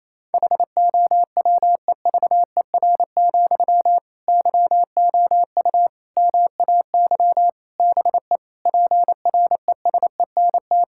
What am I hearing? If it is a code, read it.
Morse code, heard as a keyed tone, about 28 wpm, 700 Hz. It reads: HOWEVER, YOU MAY BE PRESENT
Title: Sentences from Top 1600 Words 28wpm